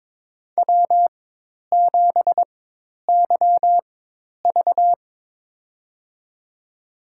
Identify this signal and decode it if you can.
Morse code, heard as a keyed tone, about 22 wpm, spaced as for 10 wpm, 700 Hz. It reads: W7YV
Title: USA Call Signs - 4 Characters - 4x Character Spacing - 22wpm